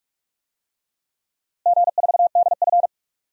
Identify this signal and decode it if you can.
Morse code, heard as a keyed tone, about 43 wpm, 700 Hz. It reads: G4DF
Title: DXCC-100 Call Signs - 4 Characters - Repeated 3x - 45wpm